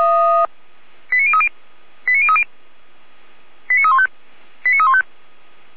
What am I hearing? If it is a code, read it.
Five-tone selective call (ZVEI), heard as quick runs of five tones, tones of about 70 ms. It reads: 89030 89030 89315 89315